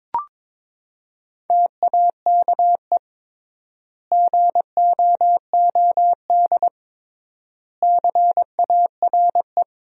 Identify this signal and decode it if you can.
Morse code, heard as a keyed tone, about 22 wpm, 700 Hz. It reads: TAKE GOOD CARE
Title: 3x Word-Spacing - Sentences from Top 400 Words - 22wpm